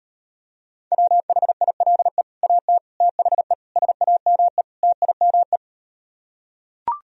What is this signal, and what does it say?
Morse code, heard as a keyed tone, about 38 wpm, 700 Hz. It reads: WHILE AT THE SAME TIME